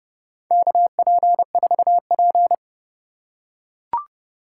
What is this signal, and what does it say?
Morse code, heard as a keyed tone, about 30 wpm, 700 Hz. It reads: KP4P
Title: USA Call Signs - 4 Characters - 3x Character Spacing - 30wpm